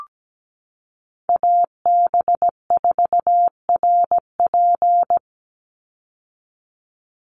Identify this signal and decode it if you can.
Morse code, heard as a keyed tone, about 17 wpm, 700 Hz. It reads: AB4RP
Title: USA Call Signs 17wpm